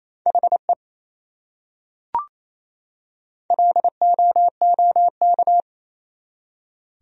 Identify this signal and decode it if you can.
Morse code, heard as a keyed tone, about 28 wpm, 700 Hz. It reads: HE LOOK
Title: Top 100 Words 28wpm